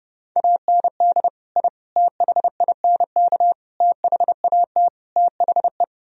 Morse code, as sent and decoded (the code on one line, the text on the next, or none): .- -. -.. / .. / - .... .. -. -.- / - .... .- - / - .... .
AND I THINK THAT THE